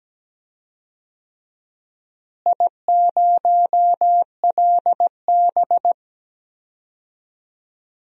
Morse code, heard as a keyed tone, about 17 wpm, 700 Hz. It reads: I0LB
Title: DXCC-100 Call Signs - 4 Characters - Repeated 2x - 17wpm